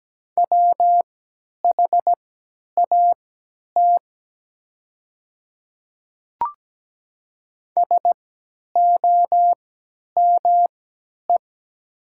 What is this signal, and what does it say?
Morse code, heard as a keyed tone, about 17 wpm, spaced as for 10 wpm, 700 Hz. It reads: WHAT SOME
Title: 3x Character Spacing - Top 100 Words - 17wpm